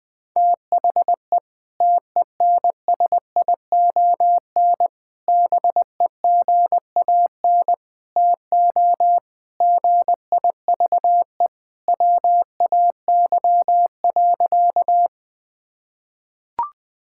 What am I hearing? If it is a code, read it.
Morse code, heard as a keyed tone, about 20 wpm, 700 Hz. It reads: THE TENSION BEGAN TO GIVE WAY.
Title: Sentences from Top 1600 Words 20wpm